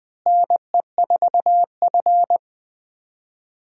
Morse code, heard as a keyed tone, about 20 wpm, 700 Hz. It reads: NE4F
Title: USA Call Signs - 4 Characters - Repeated 3x - 20wpm